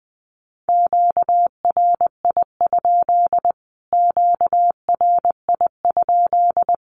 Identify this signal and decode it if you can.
Morse code, heard as a keyed tone, about 20 wpm, 700 Hz. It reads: QRI? QRI?
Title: Q-Codes Repeated 3x 20wpm